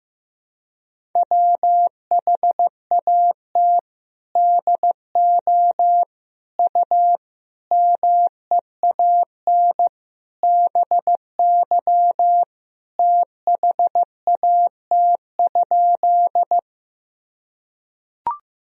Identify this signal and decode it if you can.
Morse code, heard as a keyed tone, about 16 wpm, 700 Hz. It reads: WHAT DO U MEAN BY THAT?
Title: Sentences from Top 200 Words and Abbreviations 15wpm